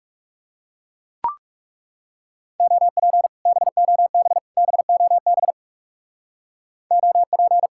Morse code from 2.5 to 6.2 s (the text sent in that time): OP BOB BOB